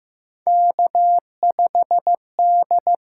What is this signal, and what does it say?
Morse code, heard as a keyed tone, about 15 wpm, 700 Hz. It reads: K5D
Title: USA Call Signs - 8x Character Spacing - 15wpm